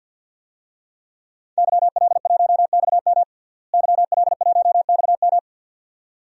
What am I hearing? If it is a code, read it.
Morse code, heard as a keyed tone, about 50 wpm, 700 Hz. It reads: YL1XM YL1XM
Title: DXCC-100 Call Signs - 5 Characters - Repeated 2x - 50wpm